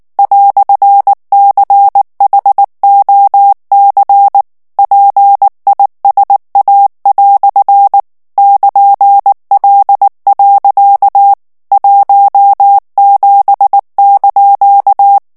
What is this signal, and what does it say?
Morse code, heard as a keyed tone, about 19 wpm, 800 Hz. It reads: "CHOC PISA" <KN>L. 17)
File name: morse mp3 / FRANCAIS TEXTE CLAIR 1 VITESSE 18 MOTS MINUTES